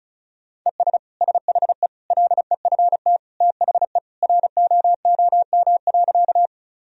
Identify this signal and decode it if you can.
Morse code, heard as a keyed tone, about 36 wpm, 700 Hz. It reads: ES SHE LEFT THE ROOM.
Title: Sentences from Top 300 Words and Abbreviations 35wpm